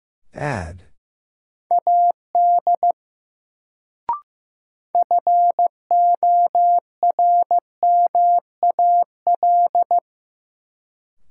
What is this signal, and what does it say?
Morse code, heard as a keyed tone, about 15 wpm, 700 Hz. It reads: AD FORMAL
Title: Sentences from Top 1300 Words - New Word Drill - 15wpm